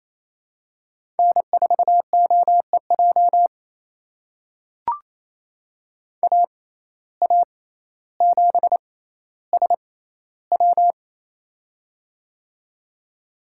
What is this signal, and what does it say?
Morse code, heard as a keyed tone, about 28 wpm, spaced as for 14 wpm, 700 Hz. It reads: N4OEJ AA7SW
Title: USA Call Signs - 5 Characters - 6x Character Spacing - 28wpm